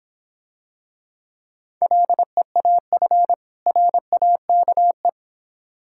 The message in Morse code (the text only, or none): LEAF RAKE